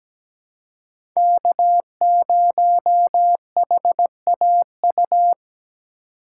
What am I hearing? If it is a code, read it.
Morse code, heard as a keyed tone, about 17 wpm, 700 Hz. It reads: K0HAU
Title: USA Call Signs - 5 Characters - 2x Character Spacing - 17wpm